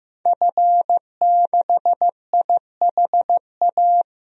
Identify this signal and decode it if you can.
Morse code, heard as a keyed tone, about 15 wpm, 700 Hz. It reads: F6IHA
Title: DXCC-100 Call Signs - 5 Characters - Repeated 3x - 15wpm